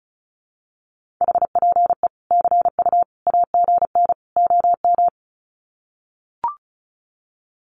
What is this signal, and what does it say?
Morse code, heard as a keyed tone, about 34 wpm, 700 Hz. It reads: HPE CU AGN OM
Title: Common QSO Phrases 35wpm